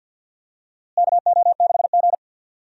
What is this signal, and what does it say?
Morse code, heard as a keyed tone, about 52 wpm, 700 Hz. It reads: KO6G
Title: USA Call Signs - 4 Characters - Repeated 3x - 50wpm